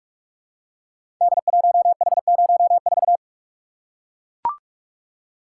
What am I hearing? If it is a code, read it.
Morse code, heard as a keyed tone, about 44 wpm, 700 Hz. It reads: D1H04
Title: Code Groups - 5 Letter:Numbers - 45wpm